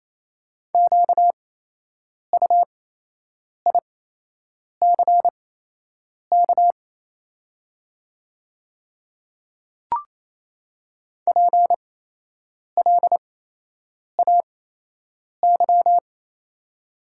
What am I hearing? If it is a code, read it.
Morse code, heard as a keyed tone, about 28 wpm, spaced as for 8 wpm, 700 Hz. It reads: QUICK PLAY
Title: 8x Character Spacing - Top 500 Words - 28wpm